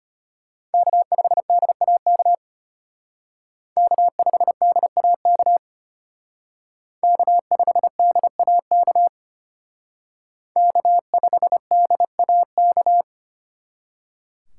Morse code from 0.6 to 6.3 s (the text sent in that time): K5DAK K5DAK